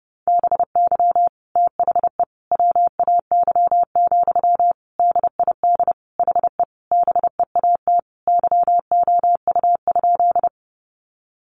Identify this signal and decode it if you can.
Morse code, heard as a keyed tone, about 30 wpm, 700 Hz. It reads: BY THE WAY, DID HE BEAT YOU?
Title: Sentences from Top 1300 Words 30wpm